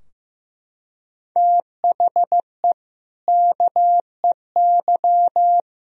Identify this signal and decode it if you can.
Morse code, heard as a keyed tone, about 15 wpm, 700 Hz. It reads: THE KEY
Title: Sentences from Top 1000 Words - Review Part 4 - 15wpm